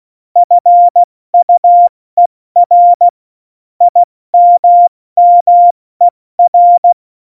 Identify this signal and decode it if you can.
Morse code, heard as a keyed tone, about 16 wpm, 700 Hz. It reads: FUER IMMER
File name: Morsekurs / Witze / Witz 27. August 2019, 70 F80